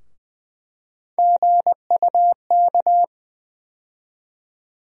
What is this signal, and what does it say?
Morse code, heard as a keyed tone, about 21 wpm, 700 Hz. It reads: GUK
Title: USA Call Sign Postfixes - 3 Letters 20wpm